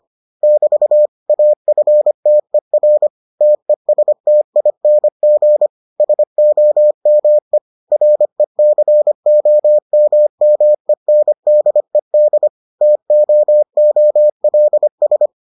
Morse code, heard as a keyed tone, about 25 wpm, 600 Hz. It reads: <BT> AFTER TESTING SOME RECOMMENDED TOOLS